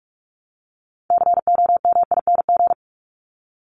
Morse code, heard as a keyed tone, about 44 wpm, 700 Hz. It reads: COMING
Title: ING Suffix - Top 100 Words with Base Words 45wpm